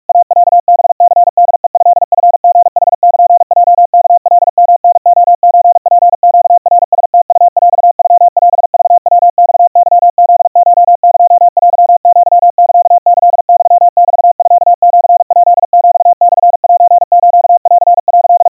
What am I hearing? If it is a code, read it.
Morse code (Japanese Wabun code), heard as a keyed tone, about 47 wpm, 700 Hz. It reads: イロハニホヘトチリヌルヲワカヨタレソツネナラムウヰノオクヤマケフコエテアサキユメミシヱヒモセスン」
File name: iroha 150 cpm 700 Hz Crispy